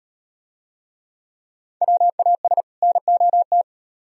Morse code, heard as a keyed tone, about 36 wpm, 700 Hz. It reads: WAS NOT